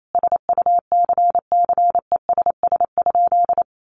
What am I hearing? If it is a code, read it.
Morse code, heard as a keyed tone, about 28 wpm, 700 Hz. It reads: SUCCESS?
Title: Sentences from Top 900 Words 28wpm